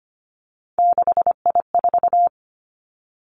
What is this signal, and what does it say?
Morse code, heard as a keyed tone, about 25 wpm, 700 Hz. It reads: 6I4
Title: Three Challenging Letter-Number Combinations 25wpm